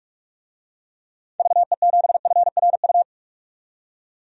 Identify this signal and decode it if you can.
Morse code, heard as a keyed tone, about 46 wpm, 700 Hz. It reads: VE7FRU
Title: DXCC-100 Call Signs - 6 Characters - 45wpm